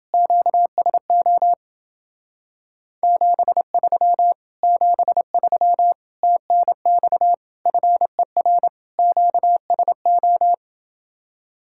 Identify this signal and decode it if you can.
Morse code, heard as a keyed tone, about 29 wpm, 700 Hz. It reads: QSO 73 73 TNX FER QSO